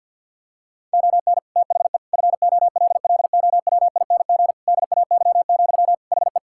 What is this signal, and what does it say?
Morse code, heard as a keyed tone, about 50 wpm, 700 Hz. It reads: ON THE FOLLOWING DAY, HE
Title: Sentences from Top 1000 Words - Review Part 6 - 50wpm